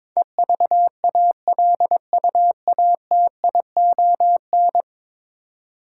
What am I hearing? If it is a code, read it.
Morse code, heard as a keyed tone, about 22 wpm, 700 Hz. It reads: EVALUATION
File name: practice / 22wpm / 10 Letter Words - First 3 Letters Given - 22wpm